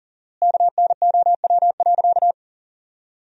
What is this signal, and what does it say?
Morse code, heard as a keyed tone, about 40 wpm, 700 Hz. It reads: KNOW.
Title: 4x Word-Spacing - Sentences from Top 400 Words - 40wpm